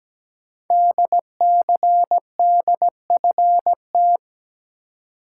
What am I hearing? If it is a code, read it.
Morse code, heard as a keyed tone, about 17 wpm, 700 Hz. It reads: DCDFT
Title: Code Groups - 5 Letters - 17wpm